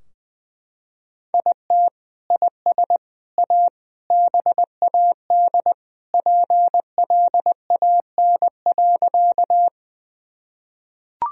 Morse code, heard as a keyed tone, about 20 wpm, 700 Hz. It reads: IT IS A BAD PLAN.
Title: Sentences from Top 1000 Words - Review Part 2 - 20wpm